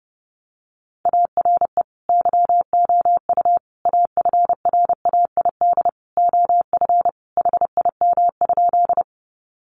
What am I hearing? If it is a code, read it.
Morse code, heard as a keyed tone, about 30 wpm, 700 Hz. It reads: ARE YOU AFRAID OF HIM?
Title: Sentences from Top 1500 Words 30wpm